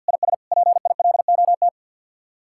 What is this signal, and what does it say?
Morse code, heard as a keyed tone, about 60 wpm, 700 Hz. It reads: IS PILOT
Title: Speed-Racing Common QSO Phrases Encore - 50wpm